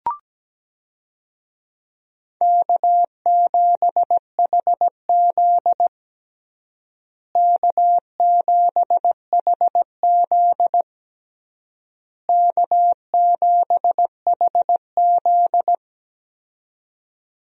Morse code, heard as a keyed tone, about 17 wpm, 700 Hz. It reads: K7HZ K7HZ K7HZ
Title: USA Call Signs - 4 Characters - Repeated 3x - 17wpm